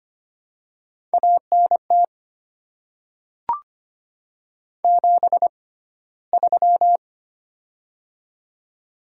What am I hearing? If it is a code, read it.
Morse code, heard as a keyed tone, about 25 wpm, spaced as for 15 wpm, 700 Hz. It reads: ANT 73
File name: Top 20 QSO Elements - 6x Character Spacing - 25wpm